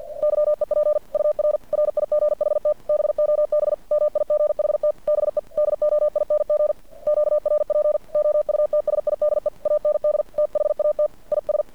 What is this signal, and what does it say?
Morse code, heard as a keyed tone, about 50 wpm, 600 Hz. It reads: YEP UR RIGHT BOB MIGHT BE DOING QRP OUTSIDE AND THAT IS
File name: archived / CRSnet